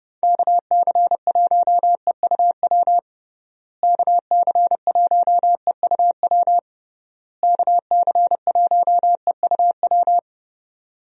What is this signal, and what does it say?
Morse code, heard as a keyed tone, about 30 wpm, 700 Hz. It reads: KC1EUW KC1EUW KC1EUW
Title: USA Call Signs - 6 Characters - Repeated 3x - 30wpm